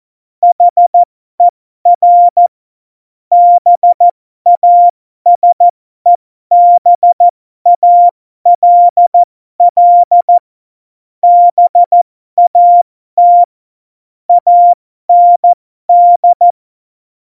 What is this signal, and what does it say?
Morse code, heard as a keyed tone, about 14 wpm, spaced as for 12 wpm, 700 Hz. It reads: HER BASEBALL BAT AND
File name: Morsekurs / Witze